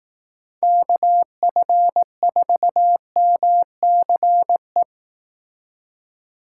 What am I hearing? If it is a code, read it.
Morse code, heard as a keyed tone, about 18 wpm, 700 Hz. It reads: KF4MCE
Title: Speed Racing - USA 2x3 Call Signs 15wpm